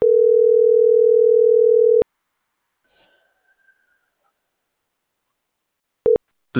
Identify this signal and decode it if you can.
Ringback tone (North America), 440 + 480 Hz.